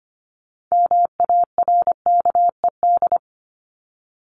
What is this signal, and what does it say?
Morse code, heard as a keyed tone, about 25 wpm, 700 Hz. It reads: MARKED
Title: ED Suffix - Top 100 Words with Base Words 25wpm